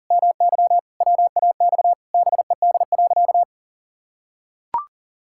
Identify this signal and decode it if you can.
Morse code, heard as a keyed tone, about 40 wpm, 700 Hz. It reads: MY WAX BED.
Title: Pangram Challenge v2 - 40wpm